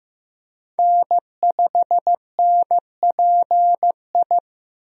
Morse code, heard as a keyed tone, about 16 wpm, 700 Hz. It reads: N5NPI
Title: USA Call Signs - 5 Characters - Repeated 2x - 15wpm